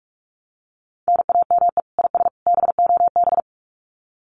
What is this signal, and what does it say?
Morse code, heard as a keyed tone, about 44 wpm, 700 Hz. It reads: NAME IS BOB
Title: Common QSO Phrases 45wpm